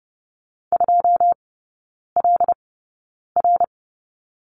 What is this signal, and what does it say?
Morse code, heard as a keyed tone, about 30 wpm, spaced as for 9 wpm, 700 Hz. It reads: 2LR
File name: course / 30:10wpm / 087 Lesson 15e - Number 5 - Three Characters 30wpm